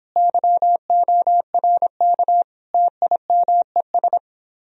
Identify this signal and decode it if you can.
Morse code, heard as a keyed tone, about 29 wpm, 700 Hz. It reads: YORK TIMES